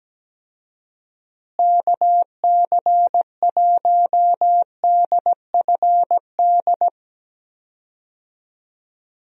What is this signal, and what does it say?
Morse code, heard as a keyed tone, about 17 wpm, 700 Hz. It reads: KC1DFD
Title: USA Call Signs - 6 Characters - Repeated 2x - 17wpm